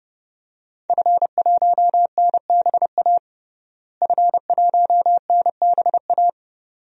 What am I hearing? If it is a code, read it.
Morse code, heard as a keyed tone, about 30 wpm, 700 Hz. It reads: F1NBA F1NBA
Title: DXCC-100 Call Signs - 5 Characters - Repeated 2x - 30wpm